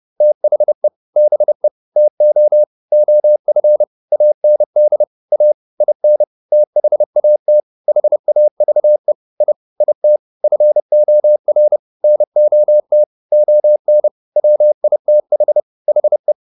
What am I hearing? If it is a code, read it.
Morse code, heard as a keyed tone, about 30 wpm, 600 Hz. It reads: THE BE TO OF AND A IN THAT HAVE I IT FOR NOT ON WITH HE
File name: Top-100-Words-30